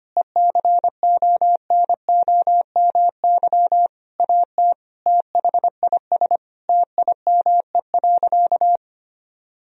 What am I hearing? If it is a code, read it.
Morse code, heard as a keyed tone, about 25 wpm, 700 Hz. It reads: ECONOMY AT THIS TIME.
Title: Sentences from Top 1000 Words - Review Part 3 - 25wpm